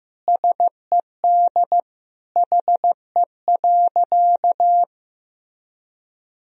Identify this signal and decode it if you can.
Morse code, heard as a keyed tone, about 15 wpm, 700 Hz. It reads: SED HE.
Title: Sentences from Top 100 Words and Abbreviations 15wpm